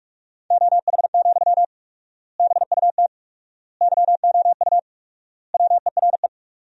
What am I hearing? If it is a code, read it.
Morse code, heard as a keyed tone, about 46 wpm, 700 Hz. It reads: OH, BUT YOU WERE
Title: 4x Word-Spacing - Sentences from Top 500 Words - 45wpm